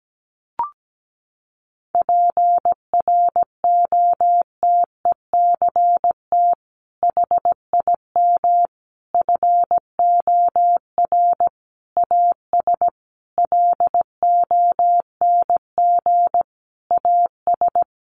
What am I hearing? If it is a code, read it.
Morse code, heard as a keyed tone, about 17 wpm, 700 Hz. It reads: PROTECT HIM FOR AS LONG AS
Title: Sentences from Top 900 Words 17wpm